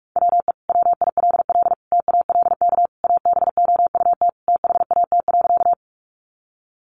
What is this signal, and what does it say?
Morse code, heard as a keyed tone, about 44 wpm, 700 Hz. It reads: WE WILL TALK ABOUT THAT.
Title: Sentences from Top 400 Words 45wpm